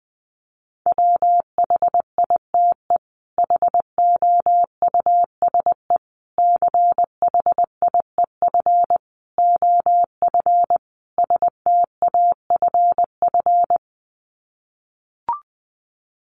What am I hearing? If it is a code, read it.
Morse code, heard as a keyed tone, about 19 wpm, 700 Hz. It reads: WHITE HOUSE CHIEF OF STAFF
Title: Sets of 5 Words Encore 20wpm